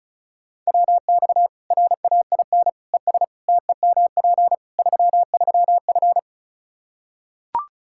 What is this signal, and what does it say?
Morse code, heard as a keyed tone, about 34 wpm, 700 Hz. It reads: WX RAIN ES TEMP 33F
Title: Practice QSO - Exchange 2 - Info - 35wpm